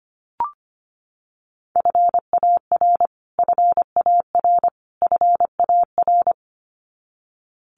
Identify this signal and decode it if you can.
Morse code, heard as a keyed tone, about 25 wpm, 700 Hz. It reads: FAR FAR FAR
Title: Top 500 Words Repeated 25wpm